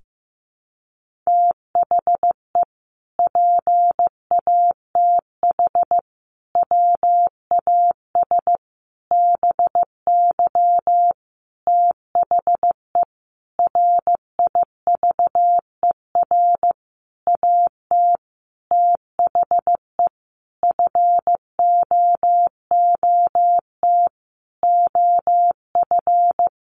Morse code, heard as a keyed tone, about 15 wpm, 700 Hz. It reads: THE PATH WAS BY THE RIVER AT THE FOOT OF
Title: Sentences from Top 1400 Words 15wpm